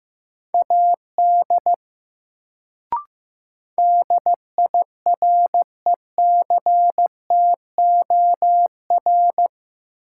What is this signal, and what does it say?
Morse code, heard as a keyed tone, about 15 wpm, 700 Hz. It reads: AD DIRECTOR
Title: Sentences from Top 1300 Words - New Word Drill Alt - 15wpm